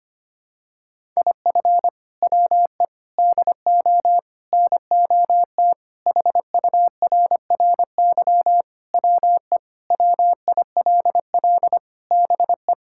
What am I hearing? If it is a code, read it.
Morse code, heard as a keyed tone, about 25 wpm, 700 Hz. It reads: IF WE DO NOT HURRY WE WILL BE
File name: practice / 25wpm / Sentences with 4-Word Phrases - 25wpm